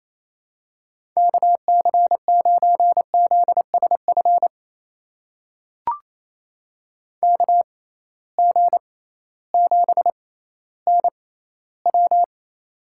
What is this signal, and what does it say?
Morse code, heard as a keyed tone, about 28 wpm, spaced as for 15 wpm, 700 Hz. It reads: KC9ZSF KG7NW
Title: USA Call Signs - 6x Character Spacing - 28wpm